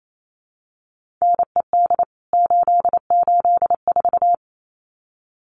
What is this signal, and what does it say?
Morse code, heard as a keyed tone, about 28 wpm, 700 Hz. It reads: NED 884
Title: CWT Exchange 28wpm